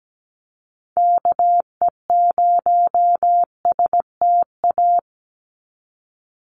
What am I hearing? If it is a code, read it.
Morse code, heard as a keyed tone, about 17 wpm, 700 Hz. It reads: KE0STA
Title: USA Call Signs 17wpm